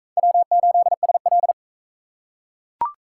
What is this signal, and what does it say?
Morse code, heard as a keyed tone, about 43 wpm, 700 Hz. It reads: W8SL